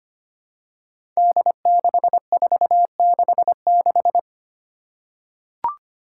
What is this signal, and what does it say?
Morse code, heard as a keyed tone, about 25 wpm, 700 Hz. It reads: D6466
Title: Five Challenging Letter-Number Combinations 25wpm